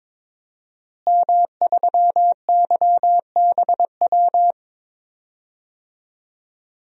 Morse code, heard as a keyed tone, about 22 wpm, 700 Hz. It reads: M3YBW
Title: DXCC-100 Call Signs - 5 Characters - Repeated 3x - 22wpm